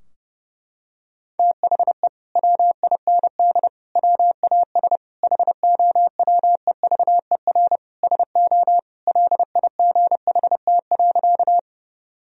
Morse code, heard as a keyed tone, about 30 wpm, 700 Hz. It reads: THE WIND WAS HOWEVER SO LIGHT.
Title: Sentences from Top 1000 Words - Review Part 2 - 30wpm